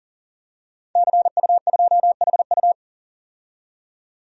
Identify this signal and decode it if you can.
Morse code, heard as a keyed tone, about 40 wpm, 700 Hz. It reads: CU2HU
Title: DXCC-100 Call Signs - 5 Characters - 40wpm